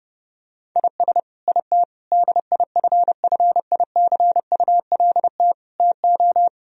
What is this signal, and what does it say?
Morse code, heard as a keyed tone, about 32 wpm, 700 Hz. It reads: IS IT DIFFICULT TO
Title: Sentences from Top 1000 Words - Review Part 3 - 30wpm